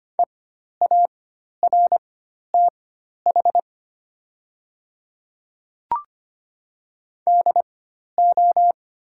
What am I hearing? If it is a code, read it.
Morse code, heard as a keyed tone, about 25 wpm, spaced as for 12 wpm, 690 Hz. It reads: EARTH DO
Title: 4x Character Spacing - Top 500 Words - 25wpm